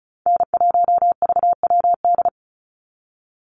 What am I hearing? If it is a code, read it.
Morse code, heard as a keyed tone, about 34 wpm, 700 Hz. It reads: N1VWD
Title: NA SOTA Chaser Callsigns 35wpm